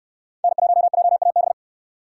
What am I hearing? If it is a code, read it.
Morse code, heard as a keyed tone, about 66 wpm, 700 Hz. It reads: N1JAB